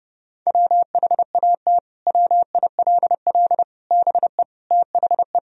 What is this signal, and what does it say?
Morse code, heard as a keyed tone, about 30 wpm, 700 Hz. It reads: WHAT WILL BE THE
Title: Sentences from Top 1000 Words - Review Part 2 - 30wpm